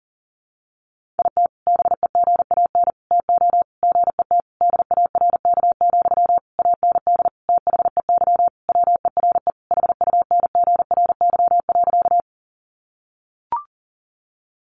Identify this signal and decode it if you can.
Morse code, heard as a keyed tone, about 40 wpm, 700 Hz. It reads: IT BEGAN TO GET DARK, AND THEY WERE HUNGRY.
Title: Sentences from Top 1400 Words 40wpm